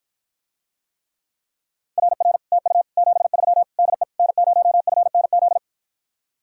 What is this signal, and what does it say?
Morse code, heard as a keyed tone, about 53 wpm, 700 Hz. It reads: RR TU 73 DE N0FNZ